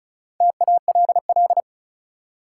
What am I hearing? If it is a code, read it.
Morse code, heard as a keyed tone, about 36 wpm, 700 Hz. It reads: TALL
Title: Four Letter Words - 35wpm